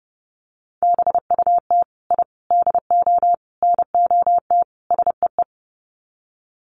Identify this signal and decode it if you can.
Morse code, heard as a keyed tone, about 30 wpm, 700 Hz. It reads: BUT I DO NOT SEE
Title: Sets of 5 Words Encore 30wpm